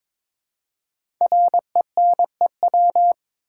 Morse code, heard as a keyed tone, about 22 wpm, 700 Hz. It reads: RENEW
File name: 5 Letter Words - First 3 Letters Given - 22wpm